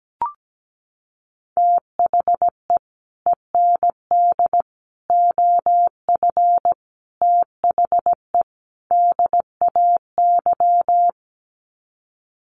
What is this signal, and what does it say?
Morse code, heard as a keyed tone, about 17 wpm, 700 Hz. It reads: THE END OF THE DAY
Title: Sets of 5 Words 17wpm